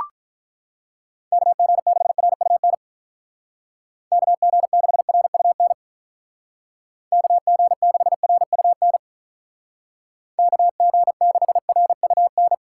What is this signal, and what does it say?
Morse code, heard as a keyed tone, about 44 wpm, 700 Hz. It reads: KG6RUN KG6RUN KG6RUN KG6RUN
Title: Speed Racing - USA 2x3 Call Signs 35wpm